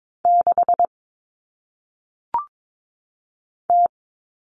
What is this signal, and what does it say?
Morse code, heard as a keyed tone, about 22 wpm, 700 Hz. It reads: 6 T
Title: All Cut Numbers - 1-Digit 22wpm